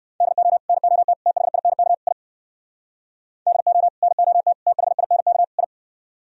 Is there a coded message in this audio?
Yes: Morse code from 0.2 to 5.6 s, heard as a keyed tone, about 62 wpm, 700 Hz. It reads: DO NOT THINK I DO NOT THINK I